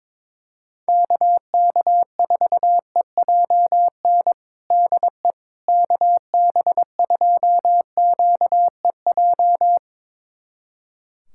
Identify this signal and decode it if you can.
Morse code, heard as a keyed tone, about 22 wpm, 700 Hz. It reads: KK4EJN DE KB2QEJ
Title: USA Callsign DE Callsign 22wpm